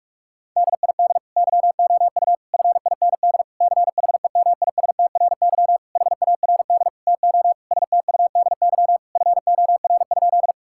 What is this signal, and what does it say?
Morse code, heard as a keyed tone, about 46 wpm, 700 Hz. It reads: DID YOU FIND CHEMISTRY HARD TO STUDY FOR?
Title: Sentences from Top 1000 Words - Review Part 6 - 45wpm